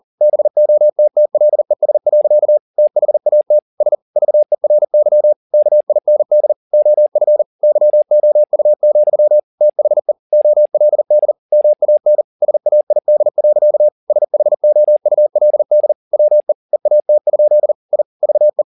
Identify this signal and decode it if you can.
Morse code, heard as a keyed tone, about 40 wpm, 600 Hz. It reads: BOTTLES. THAT S VERY KIND OF YOU, THE OLD MAN SAID. SHOULD WE EAT? I VE